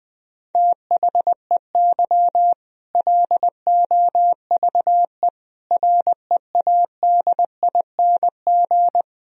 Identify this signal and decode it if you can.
Morse code, heard as a keyed tone, about 20 wpm, 700 Hz. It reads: THEY LOVE READING
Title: Sentences from Top 1000 Words - Review Part 3 - 20wpm